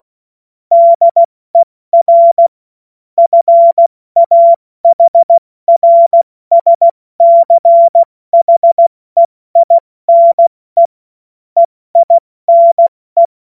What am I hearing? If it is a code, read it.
Morse code, heard as a keyed tone, about 16 wpm, spaced as for 14 wpm, 680 Hz. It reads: DER FAHRSCHEINE EINE
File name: Morsekurs / Witze / Witz 05 Dezember 2023 70 F80